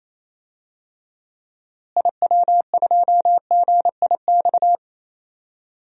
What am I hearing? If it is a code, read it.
Morse code, heard as a keyed tone, about 28 wpm, 700 Hz. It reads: IW2GIX
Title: DXCC-100 Call Signs - 6 Characters - Repeated 2x - 28wpm